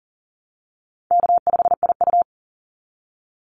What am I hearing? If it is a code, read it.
Morse code, heard as a keyed tone, about 40 wpm, 700 Hz. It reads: K5IU
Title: Easy Call Signs Repeated 2x - 40wpm